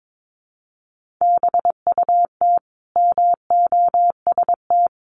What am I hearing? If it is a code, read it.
Morse code, heard as a keyed tone, about 22 wpm, 700 Hz. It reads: BUT MOST
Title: Sentences from Top 1500 Words 22wpm